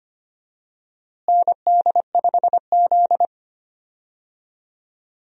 Morse code, heard as a keyed tone, about 25 wpm, 700 Hz. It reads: ND5Z
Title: USA Call Signs - 4 Characters - Repeated 2x - 25wpm